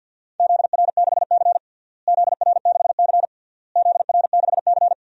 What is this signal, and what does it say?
Morse code, heard as a keyed tone, about 47 wpm, 700 Hz. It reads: ZR6C ZR6C ZR6C
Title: DXCC-100 Call Signs - 4 Characters - Repeated 3x - 50wpm